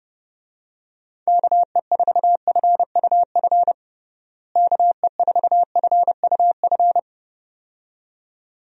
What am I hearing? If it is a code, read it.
Morse code, heard as a keyed tone, about 32 wpm, 700 Hz. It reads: KE4FUF KE4FUF
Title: USA Call Signs - 6 Characters - Repeated 2x - 30wpm